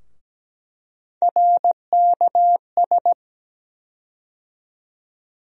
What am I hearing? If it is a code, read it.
Morse code, heard as a keyed tone, about 17 wpm, 700 Hz. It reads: RKS
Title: USA Call Sign Postfixes - 3 Letters 17wpm